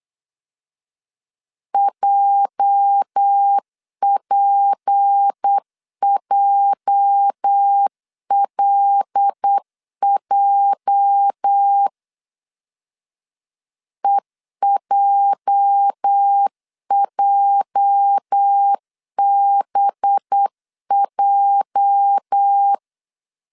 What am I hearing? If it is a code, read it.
Morse code, heard as a keyed tone, about 8 wpm, 800 Hz. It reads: JPJLJ EJJBJ
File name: LEZIONI / 8 YJ45 / 8 YJ 8wpm 3313